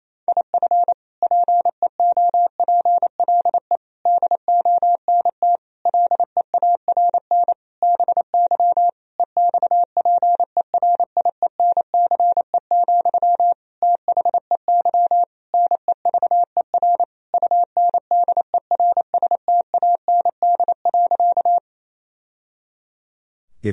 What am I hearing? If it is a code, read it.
Morse code, heard as a keyed tone, about 28 wpm, 700 Hz. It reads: IF PEOPLE DONT LEARN BY EXPERIENCE, THEY NEVER UNDERSTAND.
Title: Sentences from Top 1000 Words - Review Part 3 - 28wpm